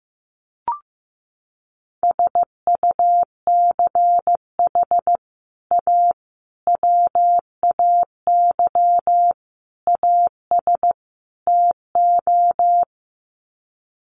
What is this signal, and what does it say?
Morse code, heard as a keyed tone, about 15 wpm, 700 Hz. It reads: SUCH A WAY AS TO
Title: Sets of 5 Words Encore2 15wpm